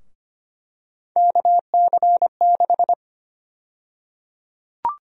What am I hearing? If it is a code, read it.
Morse code, heard as a keyed tone, about 25 wpm, 700 Hz. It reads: KC6
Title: USA Call Sign Prefixes - 25wpm